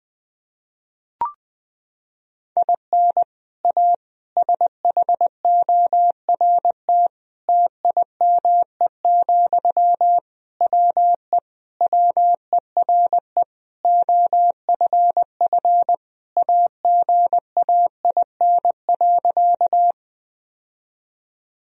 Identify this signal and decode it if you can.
Morse code, heard as a keyed tone, about 20 wpm, 700 Hz. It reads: IN A SHORT TIME, WE WERE OFF AGAIN.
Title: Sentences from Top 1000 Words - Review Part 2 - 20wpm